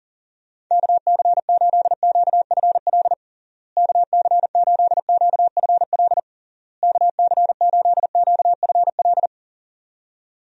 Morse code, heard as a keyed tone, about 40 wpm, 700 Hz. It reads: KC8QFL KC8QFL KC8QFL
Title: USA Call Signs - 6 Characters - Repeated 3x - 40wpm